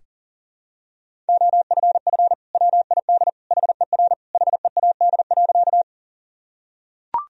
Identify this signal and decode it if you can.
Morse code, heard as a keyed tone, about 40 wpm, 700 Hz. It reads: OFF WID HER HEAD.
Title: Sentences from Top 300 Words and Abbreviations 40wpm